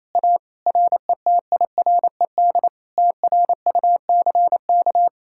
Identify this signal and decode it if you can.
Morse code, heard as a keyed tone, about 26 wpm, 700 Hz. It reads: A RETIRED TRUCK
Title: Speed-Racing Common QSO Phrases - 28wpm